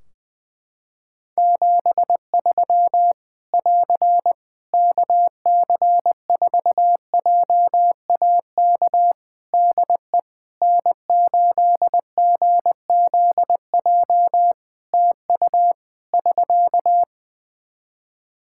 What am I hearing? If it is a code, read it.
Morse code, heard as a keyed tone, about 20 wpm, 700 Hz. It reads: 73 <AR> KC4JAK DE N8GZJ TU <SK>
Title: Practice QSO - Exchange 4 - Ending - 20wpm